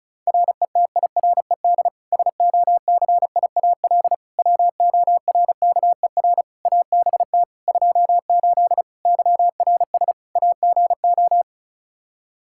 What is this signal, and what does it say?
Morse code, heard as a keyed tone, about 36 wpm, 700 Hz. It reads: RETIRED SOCIAL WORKER ABT 28 YRS AGO
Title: K1USN SST QSO Practice 35wpm